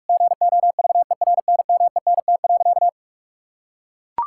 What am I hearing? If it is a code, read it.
Morse code, heard as a keyed tone, about 41 wpm, 700 Hz. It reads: GOVERNMENT.
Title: Sentences from Top 1000 Words - Review Part 6 - 45wpm